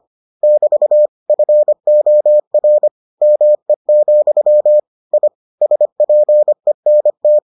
Morse code, heard as a keyed tone, about 25 wpm, 600 Hz. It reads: <BT> FOR ME, I SPENT